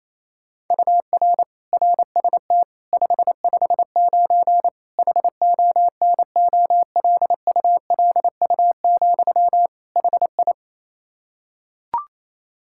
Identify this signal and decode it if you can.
Morse code, heard as a keyed tone, about 27 wpm, 700 Hz. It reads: UR RST 559 HONOLULU, HI
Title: Practice QSO - Exchange 1 - Beginning - 28wpm